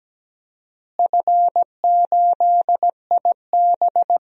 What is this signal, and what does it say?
Morse code, heard as a keyed tone, about 17 wpm, 700 Hz. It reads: F8IB
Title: DXCC-100 Call Signs - 4 Characters - Repeated 3x - 17wpm